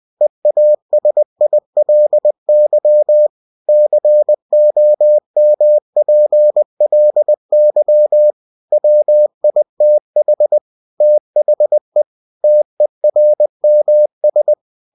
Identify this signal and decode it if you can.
Morse code, heard as a keyed tone, about 20 wpm, 600 Hz. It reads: EASILY COMPLY WITH THE TERMS
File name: cw-books / winnie-the-pooh / winnie-20wpm-page-0114